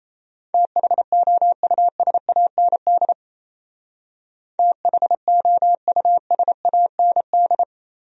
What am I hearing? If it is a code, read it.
Morse code, heard as a keyed tone, about 37 wpm, 700 Hz. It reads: THOUSAND THOUSAND